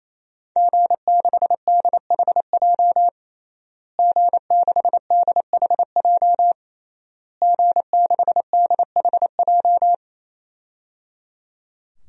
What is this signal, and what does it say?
Morse code, heard as a keyed tone, about 28 wpm, 700 Hz. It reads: G6DHJ G6DHJ G6DHJ
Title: DXCC-100 Call Signs - 5 Characters - Repeated 3x - 28wpm